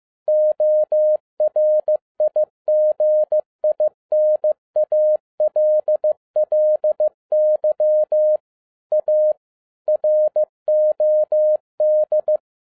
Morse code, heard as a keyed tone, about 15 wpm, 610 Hz. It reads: ORIGINALLY A ROD
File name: 2012-01-mp3 / n07-ARA JR0DIL-15wpm